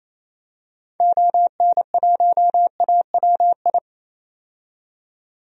Morse code, heard as a keyed tone, about 28 wpm, 700 Hz. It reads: ON1AWI
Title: DXCC-100 Call Signs - 6 Characters - 28wpm